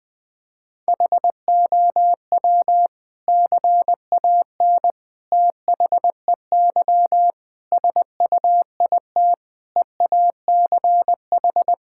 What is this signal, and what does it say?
Morse code, heard as a keyed tone, about 20 wpm, 700 Hz. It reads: HOW CAN THEY SUIT EACH
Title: Sentences from Top 1600 Words 20wpm